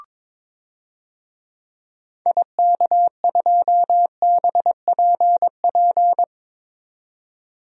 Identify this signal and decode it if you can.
Morse code, heard as a keyed tone, about 22 wpm, 700 Hz. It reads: IK2BPP
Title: DXCC-100 Call Signs - 6 Characters - 22wpm